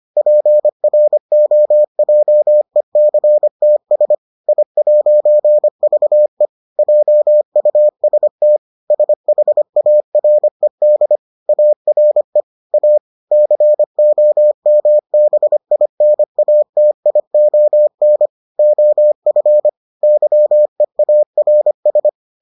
Morse code, heard as a keyed tone, about 25 wpm, 600 Hz. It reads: PROJECTS I'VE JUST SHARED ARE A COMBINATION OF YEARS